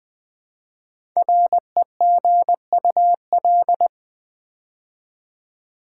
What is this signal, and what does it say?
Morse code, heard as a keyed tone, about 20 wpm, 700 Hz. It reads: REGUL
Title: Build A Word - 5 Letters At A Time - 10-Letter Words - 20wpm